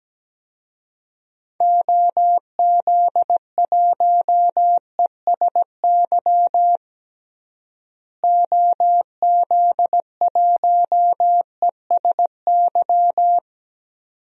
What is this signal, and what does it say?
Morse code, heard as a keyed tone, about 17 wpm, 700 Hz. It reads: OZ1ESY OZ1ESY
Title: DXCC-100 Call Signs - 6 Characters - Repeated 2x - 17wpm